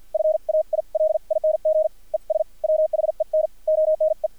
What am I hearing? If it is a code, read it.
Morse code, heard as a keyed tone, about 50 wpm, spaced as for 43 wpm, 630 Hz. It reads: FAILING ES WHEN ONE
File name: archived / CRSnet